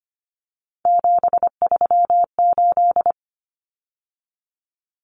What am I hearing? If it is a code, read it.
Morse code, heard as a keyed tone, about 25 wpm, 700 Hz. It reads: 738
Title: Numbers - 3-Digit 25wpm